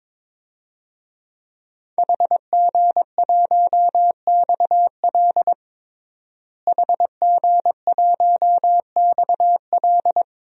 Morse code, heard as a keyed tone, about 22 wpm, 700 Hz. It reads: HG1XL HG1XL
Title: DXCC-100 Call Signs - 5 Characters - Repeated 2x - 22wpm